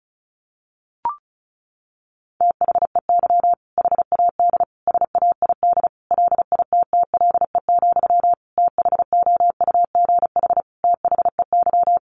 Morse code, heard as a keyed tone, about 34 wpm, 700 Hz. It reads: THEY HAD SAID LITTLE, THOUGH THEY
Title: Sentences from Top 400 Words 35wpm